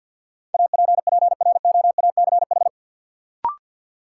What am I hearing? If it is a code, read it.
Morse code, heard as a keyed tone, about 53 wpm, 700 Hz. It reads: APPROACH
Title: Sentences from Top 1400 Words - New Word Drill Alt - 50wpm